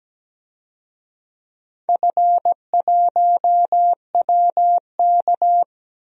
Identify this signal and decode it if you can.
Morse code, heard as a keyed tone, about 17 wpm, 700 Hz. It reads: F1WK
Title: DXCC-100 Call Signs - 4 Characters - 17wpm